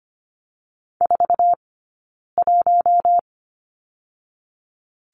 Morse code, heard as a keyed tone, about 25 wpm, 700 Hz. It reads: 4 1
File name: Lesson 15d - Number 5 - Two Characters 25wpm